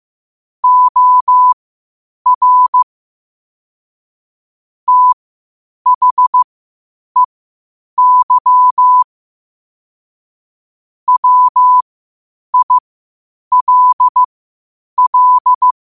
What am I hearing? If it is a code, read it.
Morse code, heard as a keyed tone, about 15 wpm, spaced as for 8 wpm, 1000 Hz. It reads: OR THEY WILL
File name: qotdfiles / 8wpm